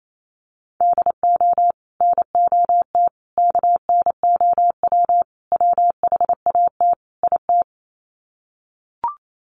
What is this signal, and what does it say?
Morse code, heard as a keyed tone, about 28 wpm, 700 Hz. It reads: DO NOT KNOW WHAT IT
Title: Sets of 5 Words 28wpm